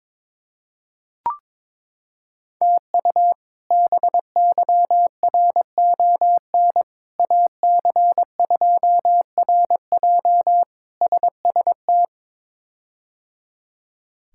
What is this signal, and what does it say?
Morse code, heard as a keyed tone, about 22 wpm, 700 Hz. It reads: TU BYRON AC2RJ SST
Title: K1USN SST QSO Practice - 22wpm